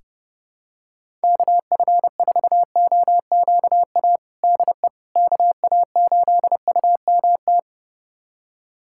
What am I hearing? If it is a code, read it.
Morse code, heard as a keyed tone, about 30 wpm, 700 Hz. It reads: KF4OQA DE KA8UMT
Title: USA Callsign DE Callsign - 6 Characters - 30wpm